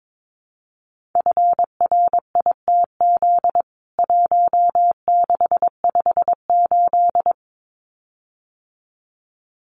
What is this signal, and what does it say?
Morse code, heard as a keyed tone, about 22 wpm, 700 Hz. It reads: FRITZ 1658
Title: CWT Exchange 22wpm